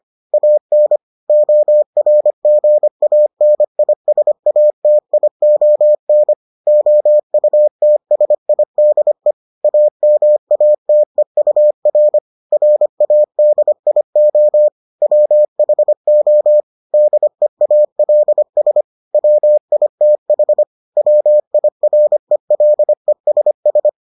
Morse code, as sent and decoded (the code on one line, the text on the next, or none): .- -. / --- .-. --. .- -. .. ... .- - .. --- -. / --- ..- - ... .. -.. / .- -- .- - . ..- .-. / .-. .- -.. .. --- / .-- .... --- / -.. .- .-.. ... / .-- .. - .... / .-- .. .-. .-.. ... ...
AN ORGANISATION OUTSIDE AMATEUR RADIO WHO DEALS WITH WIRELESS